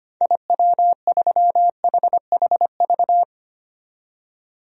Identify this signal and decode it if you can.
Morse code, heard as a keyed tone, about 25 wpm, 700 Hz. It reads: IW3HHV
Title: DXCC-100 Call Signs - 6 Characters - 25wpm